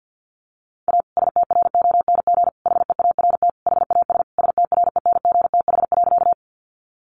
Morse code, heard as a keyed tone, about 50 wpm, 700 Hz. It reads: A STRONG HEART HAS STRENGTH.
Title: Sentences from Top 1300 Words 50wpm